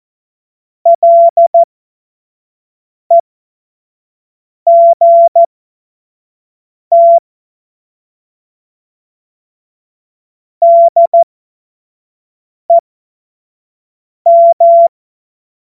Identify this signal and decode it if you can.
Morse code, heard as a keyed tone, about 14 wpm, spaced as for 5 wpm, 680 Hz. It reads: LEGT DEM
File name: Morsekurs / Witze / Witz 02. Mai 2023 25 F70